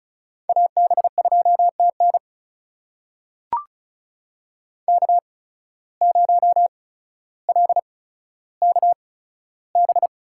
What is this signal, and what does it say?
Morse code, heard as a keyed tone, about 36 wpm, spaced as for 15 wpm, 700 Hz. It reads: AB2TN K0LKB
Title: USA Call Signs - 8x Character Spacing - 35wpm